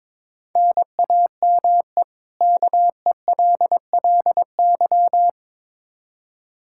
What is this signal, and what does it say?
Morse code, heard as a keyed tone, about 22 wpm, 700 Hz. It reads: NAME KELLY
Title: K1USN SST QSO Practice 22wpm